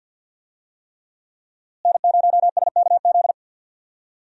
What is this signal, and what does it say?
Morse code, heard as a keyed tone, about 52 wpm, 700 Hz. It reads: N0SKZ
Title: USA Call Signs - 5 Characters - Repeated 3x - 50wpm